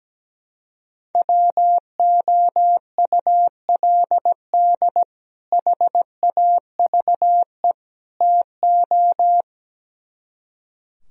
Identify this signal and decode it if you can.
Morse code, heard as a keyed tone, about 17 wpm, 700 Hz. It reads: WOULD HAVE TO
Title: Speed-Racing - Sets of 3 Words 17wpm